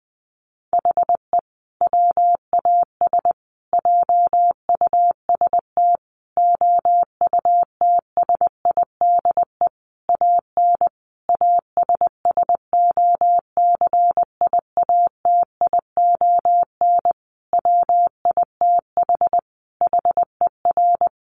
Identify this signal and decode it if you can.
Morse code, heard as a keyed tone, about 20 wpm, 700 Hz. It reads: HE WAS JUST OUTSIDE AN ASSOCIATION WITH HER
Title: Sentences from Top 1100 Words 20wpm